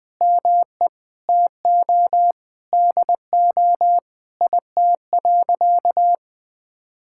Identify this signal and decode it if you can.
Morse code, heard as a keyed tone, about 20 wpm, 700 Hz. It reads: ME TO DO IT.
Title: Sentences from Top 1000 Words - Review Part 5 - 20wpm